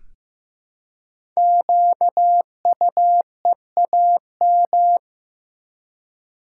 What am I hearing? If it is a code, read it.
Morse code, heard as a keyed tone, about 15 wpm, 700 Hz. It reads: QUEAM
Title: Code Groups - 5 Letters - 15wpm